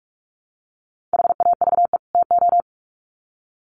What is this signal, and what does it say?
Morse code, heard as a keyed tone, about 44 wpm, 700 Hz. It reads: HAVE TO
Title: Sets of 2 Words 45wpm